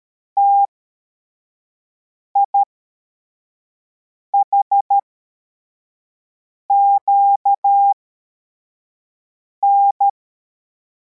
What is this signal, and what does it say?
Morse code, heard as a keyed tone, about 13 wpm, spaced as for 4 wpm, 800 Hz. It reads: TIHQN
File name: formation / cw / Text71